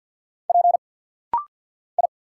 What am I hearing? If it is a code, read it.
Morse code, heard as a keyed tone, about 50 wpm, 700 Hz. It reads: P I
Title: Single Letter-Number Alt - Rapid-Fire 50wpm